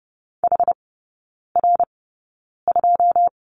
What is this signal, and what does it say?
Morse code, heard as a keyed tone, about 30 wpm, spaced as for 9 wpm, 700 Hz. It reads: HR2
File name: Lesson 18e - Letter m - Three Characters 30wpm